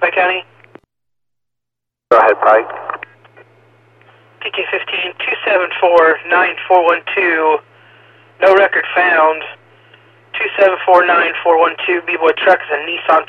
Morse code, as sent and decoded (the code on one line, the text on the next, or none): none